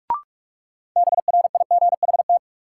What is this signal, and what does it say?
Morse code, heard as a keyed tone, about 44 wpm, 700 Hz. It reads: BRIGHT
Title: Sentences from Top 1000 Words - New Word Drill - 45wpm